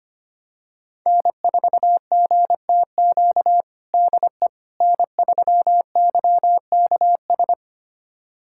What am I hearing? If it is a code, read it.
Morse code, heard as a keyed tone, about 25 wpm, 700 Hz. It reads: N4GTQ DE N3YKS
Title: USA Callsign DE Callsign - 5 Characters - 25wpm